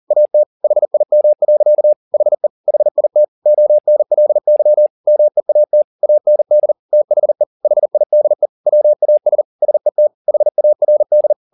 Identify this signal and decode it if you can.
Morse code, heard as a keyed tone, about 40 wpm, 600 Hz. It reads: AT HIM. HE HIT ONLY MEAT AND THE HIDE WAS SET HARD
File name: the old man and the sea 2 40wpm